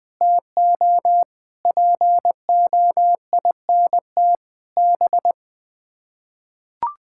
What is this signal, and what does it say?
Morse code, heard as a keyed tone, about 20 wpm, 700 Hz. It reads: TO POINT B
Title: Sets of 6 Words - 20wpm